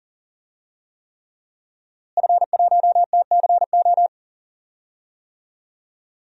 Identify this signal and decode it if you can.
Morse code, heard as a keyed tone, about 40 wpm, 700 Hz. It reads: F1TCO
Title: DXCC-100 Call Signs - 5 Characters - Repeated 3x - 40wpm